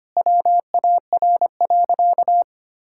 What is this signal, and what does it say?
Morse code, heard as a keyed tone, about 25 wpm, 700 Hz. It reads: WAR.
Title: Sentences from Top 1000 Words - Review Part 6 - 25wpm